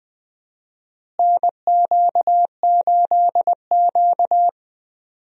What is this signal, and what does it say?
Morse code, heard as a keyed tone, about 20 wpm, 700 Hz. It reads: NQ8Q
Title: USA Call Signs - 4 Characters - Repeated 2x - 20wpm